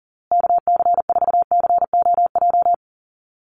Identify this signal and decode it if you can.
Morse code, heard as a keyed tone, about 40 wpm, 700 Hz. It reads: KC4COJ
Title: USA Call Signs Repeated 2x 40wpm